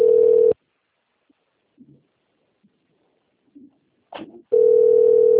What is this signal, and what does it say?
Ringback tone (North America), 440 + 480 Hz.